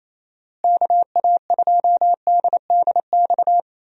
Morse code, heard as a keyed tone, about 27 wpm, 700 Hz. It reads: KA2DDX
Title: USA Call Signs - 6 Characters - Repeated 2x - 28wpm